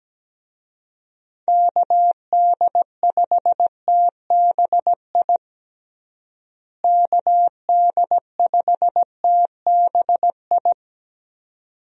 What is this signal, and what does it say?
Morse code, heard as a keyed tone, about 17 wpm, 700 Hz. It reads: KD5TBI KD5TBI
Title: USA Call Signs - 6 Characters - Repeated 2x - 17wpm